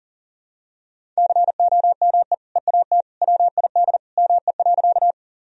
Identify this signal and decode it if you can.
Morse code, heard as a keyed tone, about 40 wpm, 700 Hz. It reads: COME EAT WID ME.
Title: Sentences from Top 300 Words and Abbreviations 40wpm